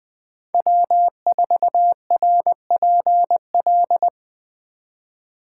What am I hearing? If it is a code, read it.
Morse code, heard as a keyed tone, about 20 wpm, 700 Hz. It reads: W4RPL
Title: Speed Racing - USA 1x3 Call Signs 17wpm